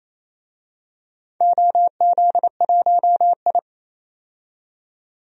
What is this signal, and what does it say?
Morse code, heard as a keyed tone, about 28 wpm, 700 Hz. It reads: OZ1I
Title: DXCC-100 Call Signs - 4 Characters - 28wpm